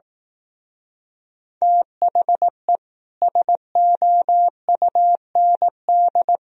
Morse code, heard as a keyed tone, about 18 wpm, 700 Hz. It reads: THE SOUND
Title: Speed-Racing - Sets of 3 Words 15wpm